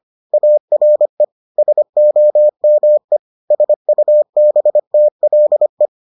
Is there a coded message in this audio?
Yes: Morse code from 0.3 to 5.9 s, heard as a keyed tone, about 25 wpm, 600 Hz. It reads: ARE SOME SUBTLE